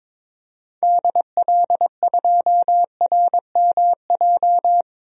Morse code, heard as a keyed tone, about 22 wpm, 700 Hz. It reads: DL2RMJ
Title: DXCC-100 Call Signs - 6 Characters - 22wpm